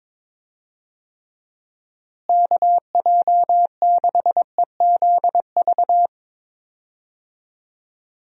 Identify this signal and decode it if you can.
Morse code, heard as a keyed tone, about 22 wpm, 700 Hz. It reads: KJ6EZV
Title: USA Call Signs - 6 Characters - Repeated 2x - 22wpm